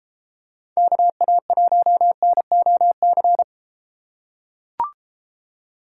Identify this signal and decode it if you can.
Morse code, heard as a keyed tone, about 33 wpm, 700 Hz. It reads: KA1NOC